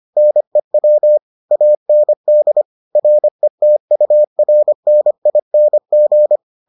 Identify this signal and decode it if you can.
Morse code, heard as a keyed tone, about 25 wpm, 600 Hz. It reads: NEW AND RETURNING